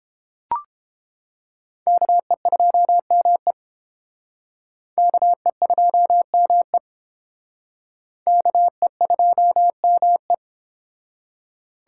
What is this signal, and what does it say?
Morse code, heard as a keyed tone, about 28 wpm, 700 Hz. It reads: KE2ME KE2ME KE2ME